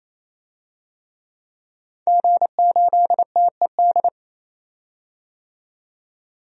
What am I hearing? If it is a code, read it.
Morse code, heard as a keyed tone, about 28 wpm, 700 Hz. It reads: G8TED
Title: DXCC-100 Call Signs - 5 Characters - 28wpm